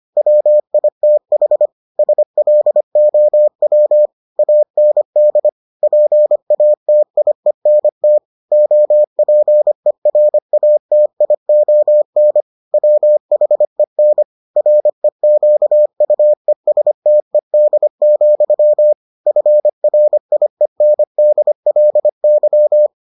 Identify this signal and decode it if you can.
Morse code, heard as a keyed tone, about 25 wpm, 600 Hz. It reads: WITH SLOW AND PATIENT OPERATION WHEN REQUESTED, FRIENDLY